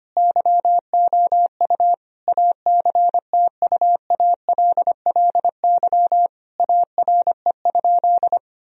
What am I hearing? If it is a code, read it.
Morse code, heard as a keyed tone, about 25 wpm, 700 Hz. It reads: YOU ACTUALLY ARE?
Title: Sentences from Top 1000 Words - Review Part 4 - 25wpm